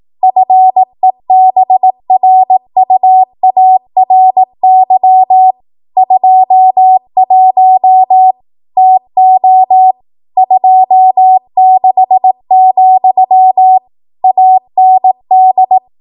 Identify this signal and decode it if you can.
Morse code, heard as a keyed tone, about 18 wpm, 750 Hz. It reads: FEBRUARY 21 TO 26, AND